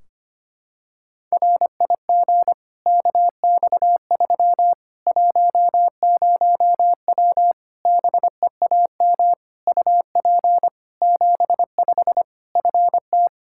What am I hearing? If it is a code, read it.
Morse code, heard as a keyed tone, about 25 wpm, 700 Hz. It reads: RIG KX3 10W BEAM UP 75 FT
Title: Practice QSO - Exchange 2 - Info - 25wpm